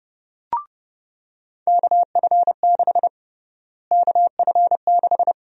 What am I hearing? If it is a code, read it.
Morse code, heard as a keyed tone, about 30 wpm, 700 Hz. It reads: KF6 KF6
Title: USA Call Sign Prefixes - Repeated 2x - 30wpm